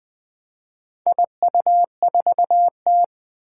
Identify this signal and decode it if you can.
Morse code, heard as a keyed tone, about 20 wpm, 700 Hz. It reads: IU4T
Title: Four Challenging Letter-Number Combinations 20wpm